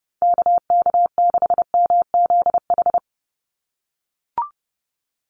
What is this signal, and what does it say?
Morse code, heard as a keyed tone, about 30 wpm, 700 Hz. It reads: KK6MZH
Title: USA Call Signs 30wpm